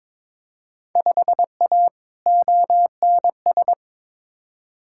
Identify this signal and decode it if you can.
Morse code, heard as a keyed tone, about 22 wpm, 700 Hz. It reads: 5A ONS
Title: Field Day Exchanges Encore - 22wpm